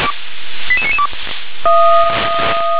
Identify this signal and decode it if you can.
Five-tone selective call (ZVEI), heard as quick runs of five tones, tones of about 70 ms. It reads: 88992
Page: SDR/ZVEI